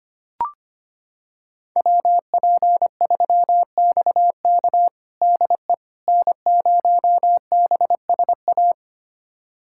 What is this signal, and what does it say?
Morse code, heard as a keyed tone, about 25 wpm, 700 Hz. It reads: WP3XK DE N0BSA
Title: USA Callsign DE Callsign - 5 Characters - 25wpm